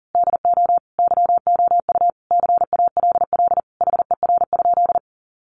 Morse code, heard as a keyed tone, about 40 wpm, 700 Hz. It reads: DO YOU CALL HER?
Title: Sentences from Top 100 Words 40wpm